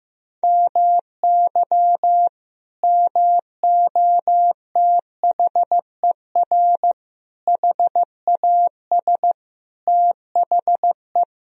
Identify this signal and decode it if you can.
Morse code, heard as a keyed tone, about 15 wpm, 700 Hz. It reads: MY MOTHER HAS THE
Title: Sentences from Top 1000 Words - Review Part 4 - 15wpm